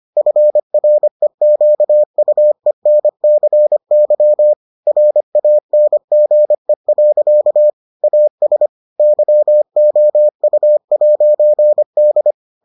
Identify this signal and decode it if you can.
Morse code, heard as a keyed tone, about 25 wpm, 600 Hz. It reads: FREQUENCY RANGE. AS YOU'D